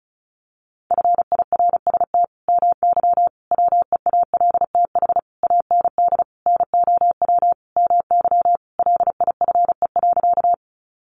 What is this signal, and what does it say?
Morse code, heard as a keyed tone, about 34 wpm, 700 Hz. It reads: FIRST MY WEALTH AND NOW MY LIFE.
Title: Sentences from Top 1200 Words 35wpm